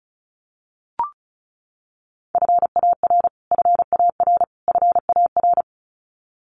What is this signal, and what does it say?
Morse code, heard as a keyed tone, about 34 wpm, 700 Hz. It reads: FAR FAR FAR
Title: Top 500 Words Repeated 35wpm